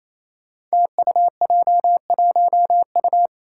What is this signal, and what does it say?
Morse code, heard as a keyed tone, about 27 wpm, 700 Hz. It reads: TUJ1U
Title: Five Challenging Letter-Number Combinations 28wpm